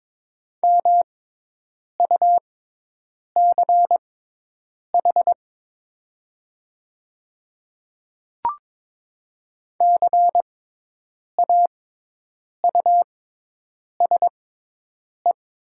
Morse code, heard as a keyed tone, about 22 wpm, spaced as for 8 wpm, 700 Hz. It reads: MUCH CAUSE
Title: 6x Character Spacing - Top 200 Words - 22wpm